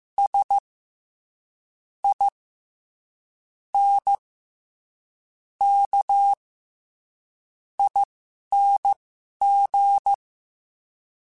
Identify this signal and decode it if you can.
Morse code, heard as a keyed tone, about 15 wpm, spaced as for 6 wpm, 800 Hz. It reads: SINKING
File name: Special INGs-mp3s / Special INGs-mp3s / SlowToFaster ing @10